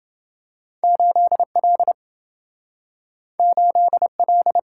Morse code, heard as a keyed tone, about 27 wpm, 700 Hz. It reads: 8L 8L